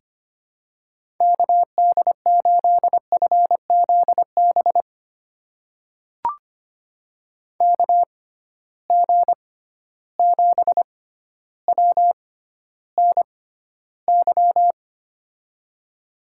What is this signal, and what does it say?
Morse code, heard as a keyed tone, about 25 wpm, spaced as for 13 wpm, 700 Hz. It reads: KD8FZB KG7WNY
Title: USA Call Signs - 6x Character Spacing - 25wpm